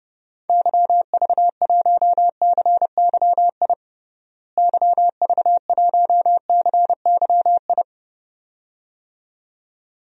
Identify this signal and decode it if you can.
Morse code, heard as a keyed tone, about 32 wpm, 700 Hz. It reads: YV1CYI YV1CYI
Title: DXCC-100 Call Signs - 6 Characters - Repeated 2x - 30wpm